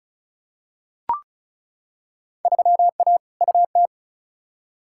Morse code, heard as a keyed tone, about 34 wpm, 700 Hz. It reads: 3A UT
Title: Field Day Exchanges - 35wpm